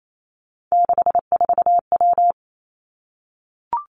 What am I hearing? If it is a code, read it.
Morse code, heard as a keyed tone, about 28 wpm, 700 Hz. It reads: 64W
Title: Three Challenging Letter-Number Combinations 28wpm